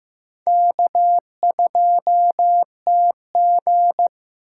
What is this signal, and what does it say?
Morse code, heard as a keyed tone, about 15 wpm, 700 Hz. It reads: K2TG
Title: USA Call Signs - 4 Characters - 8x Character Spacing - 15wpm